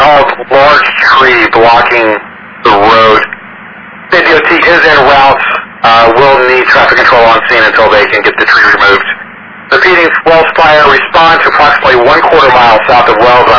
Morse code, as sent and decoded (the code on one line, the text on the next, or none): none